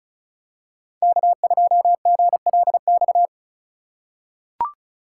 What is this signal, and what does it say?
Morse code, heard as a keyed tone, about 34 wpm, 700 Hz. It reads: K2GLX
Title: USA Call Signs - 2x Character Spacing - 35wpm